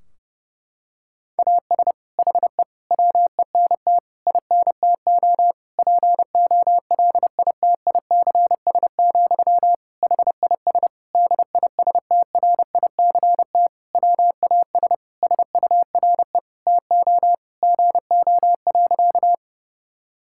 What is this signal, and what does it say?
Morse code, heard as a keyed tone, about 32 wpm, 700 Hz. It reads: AS HE WENT INTO POLITICS, HIS DISTRICT WAS SURE TO GO.
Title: Sentences from Top 1600 Words 30wpm